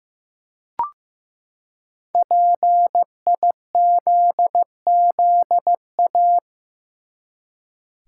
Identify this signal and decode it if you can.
Morse code, heard as a keyed tone, about 15 wpm, 700 Hz. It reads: PIZZA
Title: Five Letter Words - 15wpm